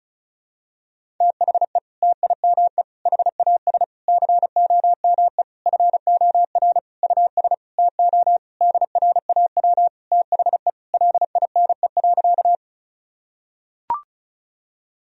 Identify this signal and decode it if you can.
Morse code, heard as a keyed tone, about 36 wpm, 700 Hz. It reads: THE TIME HAS COME FOR US TO DRAW THE LINE.
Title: Sentences from Top 1000 Words - Review Part 1 - 35wpm